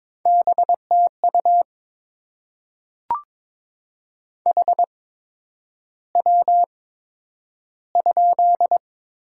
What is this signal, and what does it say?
Morse code, heard as a keyed tone, about 22 wpm, spaced as for 9 wpm, 700 Hz. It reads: BTU HW?
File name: Top 20 QSO Elements - 8x Character Spacing - 22wpm